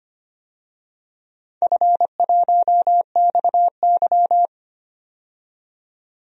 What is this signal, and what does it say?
Morse code, heard as a keyed tone, about 25 wpm, 700 Hz. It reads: F1XY
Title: DXCC-100 Call Signs - 4 Characters - 25wpm